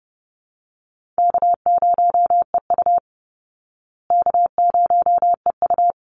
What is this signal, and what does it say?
Morse code, heard as a keyed tone, about 30 wpm, 700 Hz. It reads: K0EU K0EU
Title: Easy Call Signs Repeated 2x - 30wpm